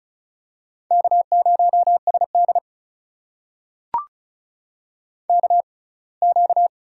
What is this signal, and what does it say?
Morse code, heard as a keyed tone, about 36 wpm, spaced as for 24 wpm, 700 Hz. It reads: K0SD KQ
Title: USA Call Signs - 4 Characters - 6x Character Spacing - 35wpm